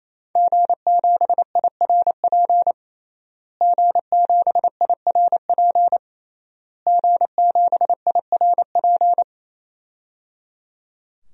Morse code, heard as a keyed tone, about 28 wpm, 700 Hz. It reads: G7IRP G7IRP G7IRP
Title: DXCC-100 Call Signs - 5 Characters - Repeated 3x - 28wpm